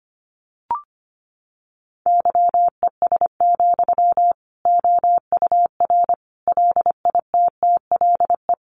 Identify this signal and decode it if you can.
Morse code, heard as a keyed tone, about 25 wpm, 700 Hz. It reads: YES, OUR LITTLE
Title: Sentences from Top 700 Words 25wpm